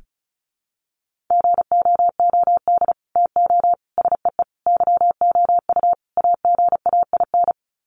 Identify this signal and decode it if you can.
Morse code, heard as a keyed tone, about 34 wpm, 700 Hz. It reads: GOOD TO SEE YOU AGAIN
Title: Sets of 5 Words Encore2 35wpm